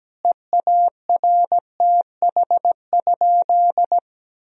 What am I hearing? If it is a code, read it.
Morse code, heard as a keyed tone, about 17 wpm, 700 Hz. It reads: EARTH?
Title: Sentences from Top 1000 Words - Review Part 3 - 17wpm